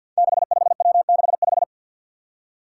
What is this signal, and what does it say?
Morse code, heard as a keyed tone, about 52 wpm, 700 Hz. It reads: 65W65
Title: Five Challenging Letter-Number Combinations 50wpm